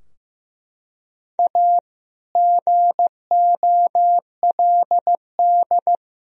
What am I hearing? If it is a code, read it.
Morse code, heard as a keyed tone, about 16 wpm, 700 Hz. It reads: A GOLD
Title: Sentences from Top 1000 Words - Review Part 3 - 15wpm